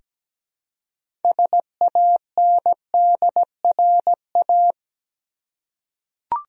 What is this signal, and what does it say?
Morse code, heard as a keyed tone, about 17 wpm, 700 Hz. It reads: SANDRA
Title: USA Common Names - 17wpm